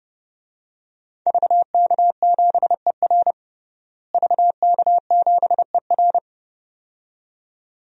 Morse code, heard as a keyed tone, about 30 wpm, 700 Hz. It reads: VK7ER VK7ER
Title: DXCC-100 Call Signs - 5 Characters - Repeated 2x - 30wpm